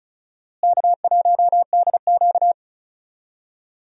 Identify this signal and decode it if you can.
Morse code, heard as a keyed tone, about 34 wpm, 700 Hz. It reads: K1DQ
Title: USA Call Signs - 4 Characters - Repeated 3x - 35wpm